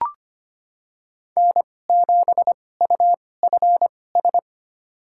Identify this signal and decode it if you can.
Morse code, heard as a keyed tone, about 25 wpm, spaced as for 18 wpm, 700 Hz. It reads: N7UFS
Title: USA Call Signs - 5 Characters - 2x Character Spacing - 25wpm